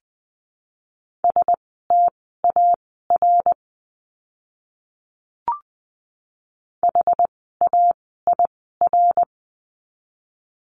Morse code, heard as a keyed tone, about 20 wpm, spaced as for 14 wpm, 700 Hz. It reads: STAR HAIR
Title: 101 Lesson 17f - Letter u - Words 20wpm